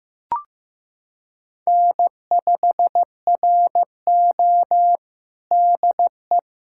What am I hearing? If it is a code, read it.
Morse code, heard as a keyed tone, about 15 wpm, 700 Hz. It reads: N5RO DE
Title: USA Callsign DE Callsign - 4 Characters - 15wpm